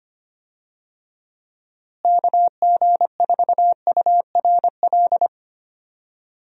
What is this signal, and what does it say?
Morse code, heard as a keyed tone, about 25 wpm, 700 Hz. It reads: KG4URL
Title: USA Call Signs - 6 Characters - Repeated 2x - 25wpm